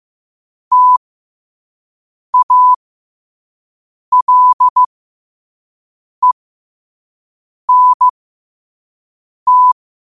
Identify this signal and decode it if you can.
Morse code, heard as a keyed tone, about 15 wpm, spaced as for 5 wpm, 1000 Hz. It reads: TALENT
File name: qotdfiles / 5wpm